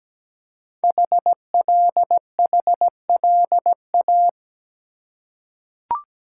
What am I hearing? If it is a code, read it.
Morse code, heard as a keyed tone, about 17 wpm, 700 Hz. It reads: HLHLA
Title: Code Groups - 5 Letters - 17wpm